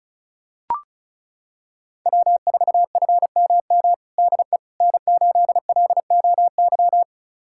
Morse code, heard as a keyed tone, about 34 wpm, 700 Hz. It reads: W4FMM DE N8LOY
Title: USA Callsign DE Callsign - 5 Characters - 35wpm